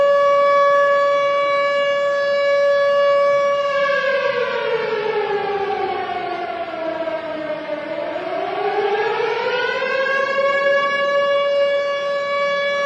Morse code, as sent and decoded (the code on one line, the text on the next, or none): none